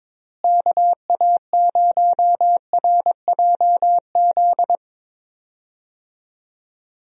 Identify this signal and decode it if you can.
Morse code, heard as a keyed tone, about 22 wpm, 700 Hz. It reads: KA0RJZ
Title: USA Call Signs - 6 Characters - Repeated 3x - 22wpm